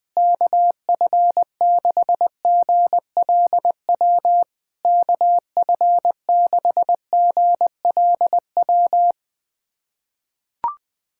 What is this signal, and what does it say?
Morse code, heard as a keyed tone, about 20 wpm, 690 Hz. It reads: KF6GLW KF6GLW
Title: SOTA QSO Practice 20wpm